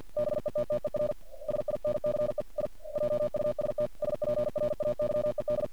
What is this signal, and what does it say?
Morse code, heard as a keyed tone, about 50 wpm, 600 Hz. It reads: BETTER SINCE I JUST SPRAYED
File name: archived / CRSnet